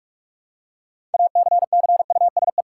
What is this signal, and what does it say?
Morse code, heard as a keyed tone, about 46 wpm, 700 Hz. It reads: ACCUSE
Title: Six Letter Words - 45wpm